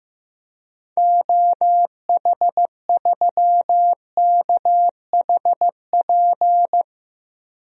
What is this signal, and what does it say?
Morse code, heard as a keyed tone, about 15 wpm, 700 Hz. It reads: OH3KHP
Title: DXCC-100 Call Signs - 6 Characters - 15wpm